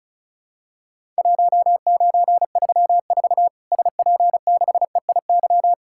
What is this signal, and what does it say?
Morse code, heard as a keyed tone, about 34 wpm, 700 Hz. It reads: 1934 SP6EIY
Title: Worked All Europe DX Contest - QTC Exchange - 35wpm